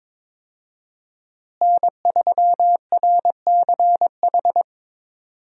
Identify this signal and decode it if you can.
Morse code, heard as a keyed tone, about 22 wpm, 700 Hz. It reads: N3RCH
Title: USA Call Signs - 5 Characters - Repeated 3x - 22wpm